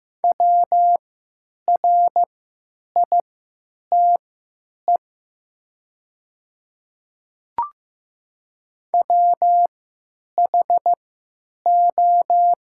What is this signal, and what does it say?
Morse code, heard as a keyed tone, about 15 wpm, spaced as for 9 wpm, 690 Hz. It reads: WRITE WHO
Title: 3x Character Spacing - Top 100 Words - 15wpm